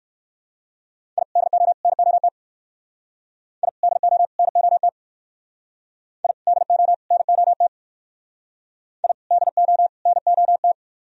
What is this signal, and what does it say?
Morse code, heard as a keyed tone, about 58 wpm, 700 Hz. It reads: I DO NOT I DO NOT I DO NOT I DO NOT